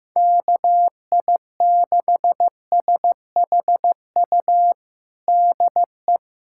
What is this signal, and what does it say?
Morse code, heard as a keyed tone, about 15 wpm, 700 Hz. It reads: KI6SHU DE
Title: USA Callsign DE Callsign - 6 Characters - 15wpm